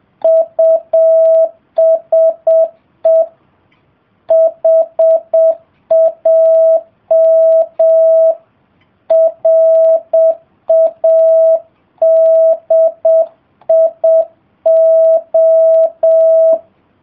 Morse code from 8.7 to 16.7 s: RADIO